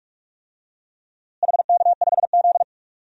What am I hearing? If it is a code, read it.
Morse code, heard as a keyed tone, about 45 wpm, 700 Hz. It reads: HK5Z
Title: DXCC-100 Call Signs - 4 Characters - Repeated 2x - 45wpm